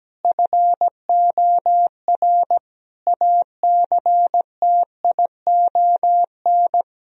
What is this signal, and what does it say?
Morse code, heard as a keyed tone, about 17 wpm, 700 Hz. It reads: FOR ACTION
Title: Sentences from Top 1000 Words - Review Part 4 - 17wpm